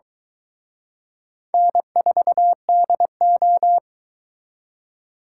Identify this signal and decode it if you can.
Morse code, heard as a keyed tone, about 21 wpm, 700 Hz. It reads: N4DO
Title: Speed Racing - USA 1x2 Call Signs 17wpm